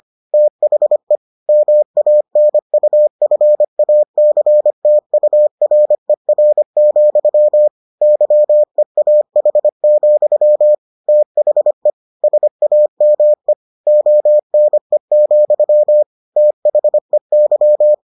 Morse code, heard as a keyed tone, about 25 wpm, 600 Hz. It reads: THE MANUFACTURER, YEAH, THE SAME ONE, THEY